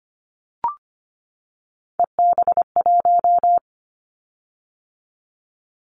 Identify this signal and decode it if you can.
Morse code, heard as a keyed tone, about 25 wpm, 700 Hz. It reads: EB1
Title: Callsign Prefixes for Top 25 DXCC - 25wpm